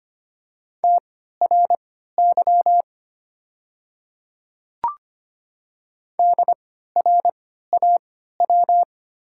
Morse code, heard as a keyed tone, about 25 wpm, spaced as for 14 wpm, 700 Hz. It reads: TRY DRAW
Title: 3x Character Spacing - Top 500 Words - 25wpm